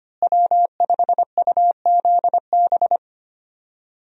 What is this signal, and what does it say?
Morse code, heard as a keyed tone, about 25 wpm, 700 Hz. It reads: W5UZB
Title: USA Call Signs - 5 Characters - Repeated 3x - 25wpm